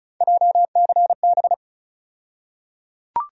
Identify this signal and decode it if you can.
Morse code, heard as a keyed tone, about 34 wpm, 700 Hz. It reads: JCB
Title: USA Call Sign Postfixes - 3 Letters 35wpm